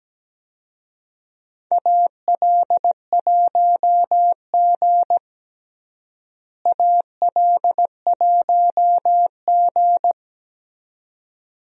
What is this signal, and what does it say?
Morse code, heard as a keyed tone, about 17 wpm, 700 Hz. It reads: AL1G AL1G
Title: USA Call Signs - 4 Characters - Repeated 2x - 17wpm